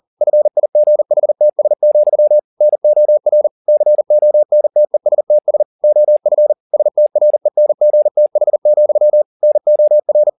Morse code, heard as a keyed tone, about 40 wpm, 600 Hz. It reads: FIGHTS, NOR CONTESTS OF STRENGTH, NOR